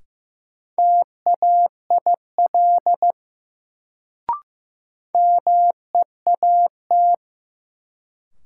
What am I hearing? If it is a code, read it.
Morse code, heard as a keyed tone, about 15 wpm, 700 Hz. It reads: TAIL MEAT
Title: Sentences from Top 1000 Words - Review Part 3 - New Word Drill - 15wpm